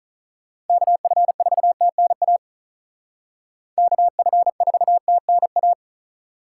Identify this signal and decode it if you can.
Morse code, heard as a keyed tone, about 44 wpm, 700 Hz. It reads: KF4TNA KF4TNA